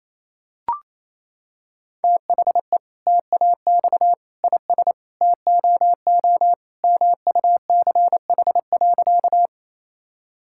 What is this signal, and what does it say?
Morse code, heard as a keyed tone, about 28 wpm, 700 Hz. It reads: THE TAX IS TOO MUCH.
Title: Sentences from Top 1000 Words - Review Part 5 - 28wpm